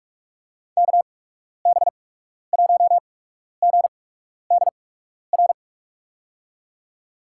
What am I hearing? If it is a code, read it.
Morse code, heard as a keyed tone, about 46 wpm, spaced as for 12 wpm, 700 Hz. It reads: KB1GDR
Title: USA Call Signs - 6 Characters - 8x Character Spacing - 45wpm